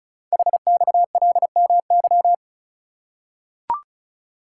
Morse code, heard as a keyed tone, about 36 wpm, 690 Hz. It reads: HXLMY
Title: Code Groups - 5 Letters - 35wpm